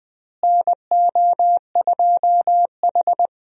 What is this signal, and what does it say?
Morse code, heard as a keyed tone, about 20 wpm, 700 Hz. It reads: NO2H
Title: USA Call Signs - 4 Characters - Repeated 2x - 20wpm